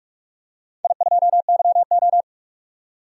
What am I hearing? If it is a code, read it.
Morse code, heard as a keyed tone, about 46 wpm, 700 Hz. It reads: I2YO
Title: DXCC-100 Call Signs - 4 Characters - 45wpm